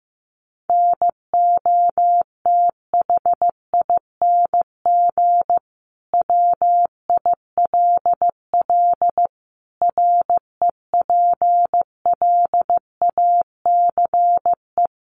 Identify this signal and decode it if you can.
Morse code, heard as a keyed tone, about 15 wpm, 700 Hz. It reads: NOTHING WILL REPLACE
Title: Sentences from Top 1400 Words 15wpm